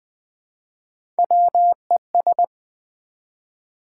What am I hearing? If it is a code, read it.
Morse code, heard as a keyed tone, about 20 wpm, 700 Hz. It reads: WES
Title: Build A Word - 1 Letter At A Time - 4-Letter Words - 20wpm